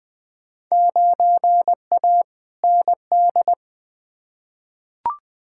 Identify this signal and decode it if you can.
Morse code, heard as a keyed tone, about 20 wpm, 700 Hz. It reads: 9A ND
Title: Field Day Exchanges Encore - 20wpm